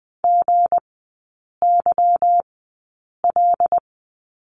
Morse code, heard as a keyed tone, about 20 wpm, spaced as for 8 wpm, 690 Hz. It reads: GYL
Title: 159 Lesson 26e - Letter g - Three Characters 20wpm